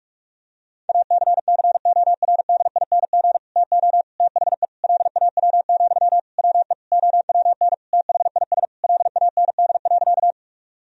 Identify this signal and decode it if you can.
Morse code, heard as a keyed tone, about 46 wpm, 700 Hz. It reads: ACCORDING TO THE LAW, WE OWN THIS LAND.
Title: Sentences from Top 1000 Words - Review Part 4 - 45wpm